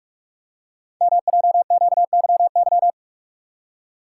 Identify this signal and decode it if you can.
Morse code, heard as a keyed tone, about 44 wpm, 700 Hz. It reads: MJQYY
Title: Code Groups - 5 Letters - 45wpm